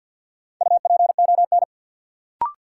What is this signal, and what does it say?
Morse code, heard as a keyed tone, about 53 wpm, 700 Hz. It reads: UPON
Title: Sentences from Top 1100 Words - New Word Drill Alt - 50wpm